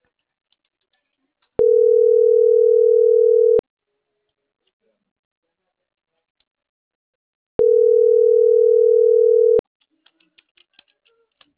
Ringback tone (North America), 440 + 480 Hz.
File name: CENTRAL MAYAB / out esmeralda.garcia